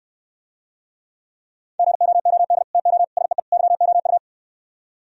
Morse code, heard as a keyed tone, about 68 wpm, 700 Hz. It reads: GOOD TO SEE YOU